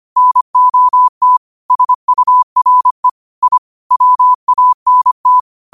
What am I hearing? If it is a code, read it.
Morse code, heard as a keyed tone, about 25 wpm, 1000 Hz. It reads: NOT SURE I WANT
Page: Quotes for Wed, 13 Aug 2025 in Morse Code at 25 words per minute.